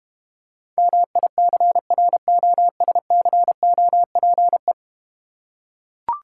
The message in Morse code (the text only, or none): MICROSCOPE